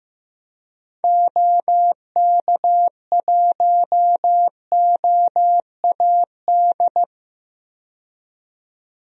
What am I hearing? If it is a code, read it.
Morse code, heard as a keyed tone, about 15 wpm, 700 Hz. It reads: OK1OAD
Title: DXCC-100 Call Signs - 6 Characters - 15wpm